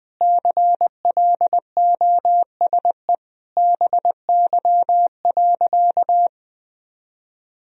Morse code, heard as a keyed tone, about 20 wpm, 700 Hz. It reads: CLOSE BY.
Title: Sentences from Top 1000 Words - Review Part 4 - 20wpm